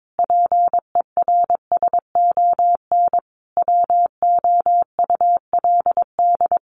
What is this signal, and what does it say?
Morse code, heard as a keyed tone, about 22 wpm, 700 Hz. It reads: PERSON WOULD
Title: Sentences from Top 1100 Words 22wpm